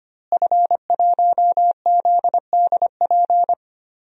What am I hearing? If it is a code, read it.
Morse code, heard as a keyed tone, about 25 wpm, 700 Hz. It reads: F1ZDP
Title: DXCC-100 Call Signs - 5 Characters - Repeated 3x - 25wpm